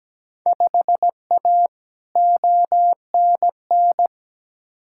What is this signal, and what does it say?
Morse code, heard as a keyed tone, about 17 wpm, 700 Hz. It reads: 5A ONN
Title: Field Day Exchanges Encore - 17wpm